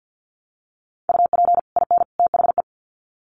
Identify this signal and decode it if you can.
Morse code, heard as a keyed tone, about 50 wpm, 700 Hz. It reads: UP IN THE
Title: Sets of 3 Words 50wpm